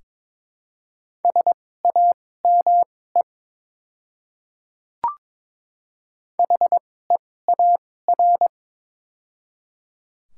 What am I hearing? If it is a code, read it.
Morse code, heard as a keyed tone, about 22 wpm, spaced as for 16 wpm, 700 Hz. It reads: SAME HEAR
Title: 2x Character Spacing - Top 300 Words - 22wpm